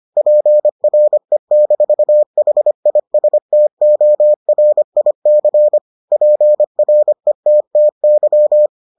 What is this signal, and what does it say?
Morse code, heard as a keyed tone, about 25 wpm, 600 Hz. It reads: PRE-HISTORIC PRETTY